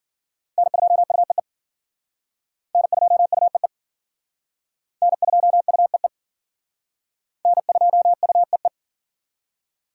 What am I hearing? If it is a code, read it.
Morse code, heard as a keyed tone, about 49 wpm, 700 Hz. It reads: N2UEE N2UEE N2UEE N2UEE
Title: Speed Racing - USA 1x3 Call Signs 40wpm